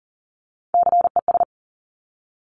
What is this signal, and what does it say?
Morse code, heard as a keyed tone, about 42 wpm, 700 Hz. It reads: CES
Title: Top Three Letter Combinations in Words 40wpm